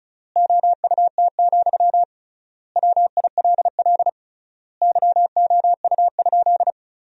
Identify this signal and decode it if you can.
Morse code, heard as a keyed tone, about 36 wpm, 700 Hz. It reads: OUT, WILL YOU?
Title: 3x Word-Spacing - Sentences from Top 300 Words - 35wpm